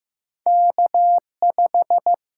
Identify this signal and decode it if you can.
Morse code, heard as a keyed tone, about 15 wpm, 700 Hz. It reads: K5
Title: USA Call Signs - 6x Character Spacing - 15wpm